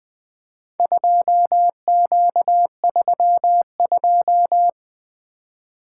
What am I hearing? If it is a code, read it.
Morse code, heard as a keyed tone, about 20 wpm, 700 Hz. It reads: 2Q32
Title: Four Letter-Number Combinations - 20wpm